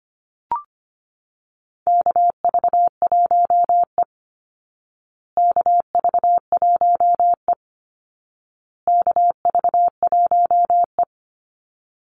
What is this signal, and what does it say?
Morse code, heard as a keyed tone, about 25 wpm, 700 Hz. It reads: KV1E KV1E KV1E
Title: Call Signs Repeated 2x - 25wpm